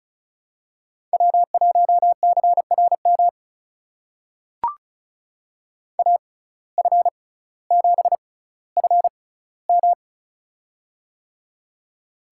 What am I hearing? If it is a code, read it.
Morse code, heard as a keyed tone, about 36 wpm, spaced as for 18 wpm, 700 Hz. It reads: W1CRM AF7FM
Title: USA Call Signs - 5 Characters - 6x Character Spacing - 35wpm